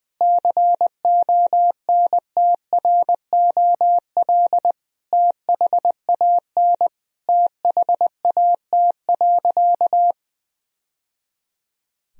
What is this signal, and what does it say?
Morse code, heard as a keyed tone, about 20 wpm, 700 Hz. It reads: CONTROL THAN THAT.
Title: Sentences from Top 1000 Words - Review Part 3 - 20wpm